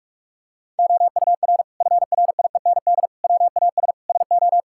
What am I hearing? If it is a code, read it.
Morse code, heard as a keyed tone, about 46 wpm, 700 Hz. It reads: OUR FRIEND WAS SO
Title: Sentences from Top 1000 Words - Review Part 5 - 45wpm